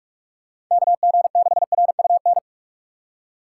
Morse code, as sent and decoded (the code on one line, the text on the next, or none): -.- --. -.... .-. ..- -.
KG6RUN